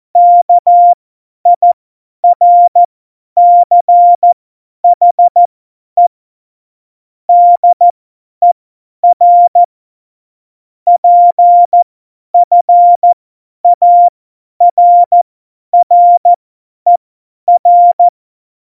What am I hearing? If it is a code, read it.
Morse code, heard as a keyed tone, about 14 wpm, spaced as for 10 wpm, 700 Hz. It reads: KIRCHE DER PFARRER
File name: Morsekurs / Witze / Witz 13. August 2019, 50 F70